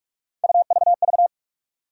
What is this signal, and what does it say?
Morse code, heard as a keyed tone, about 46 wpm, 700 Hz. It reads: UVV
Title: USA Call Sign Postfixes - 3 Letters 45wpm